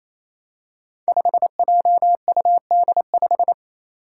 Morse code, heard as a keyed tone, about 28 wpm, 700 Hz. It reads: 5JUD5
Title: Five Challenging Letter-Number Combinations 28wpm